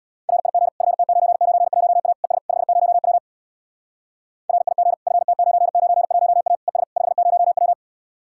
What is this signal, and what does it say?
Morse code, heard as a keyed tone, about 62 wpm, 700 Hz. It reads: RIG FT991A ES 50W RIG FT991A ES 50W
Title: Speed-Racing Common QSO Phrases - 50wpm